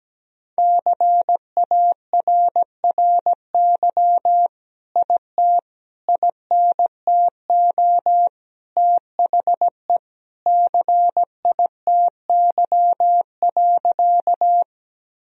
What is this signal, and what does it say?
Morse code, heard as a keyed tone, about 17 wpm, 700 Hz. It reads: CARRY IT INTO THE CITY.
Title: Sentences from Top 1000 Words - Review Part 1 - 17wpm